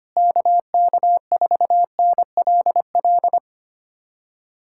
Morse code, heard as a keyed tone, about 25 wpm, 700 Hz. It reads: KK4NLL
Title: USA Callsign DE Callsign - 6 Characters - 25wpm